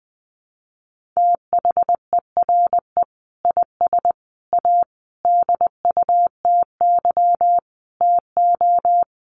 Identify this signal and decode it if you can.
Morse code, heard as a keyed tone, about 20 wpm, 700 Hz. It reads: THERE IS A DUTY TO
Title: Sentences from Top 1500 Words 20wpm